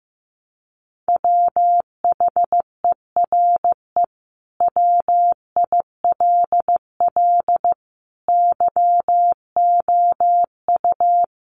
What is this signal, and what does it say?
Morse code, heard as a keyed tone, about 15 wpm, 700 Hz. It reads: WHERE WILL YOU
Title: Sentences from Top 900 Words 15wpm